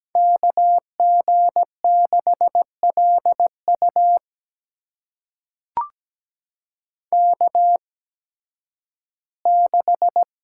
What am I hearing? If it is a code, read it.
Morse code, heard as a keyed tone, about 17 wpm, spaced as for 11 wpm, 700 Hz. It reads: KG6LU K6
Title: USA Call Signs - 5 Characters - 8x Character Spacing - 17wpm